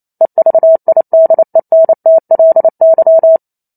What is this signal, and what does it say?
Morse code, heard as a keyed tone, about 28 wpm, 650 Hz. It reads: EVIDENTLY
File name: Chapter 2 - 28 WPM